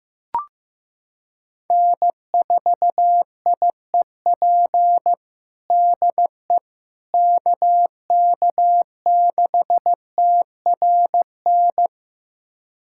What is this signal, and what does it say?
Morse code, heard as a keyed tone, about 15 wpm, 700 Hz. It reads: N4IEP DE KK6TRN
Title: USA Callsign DE Callsign 15wpm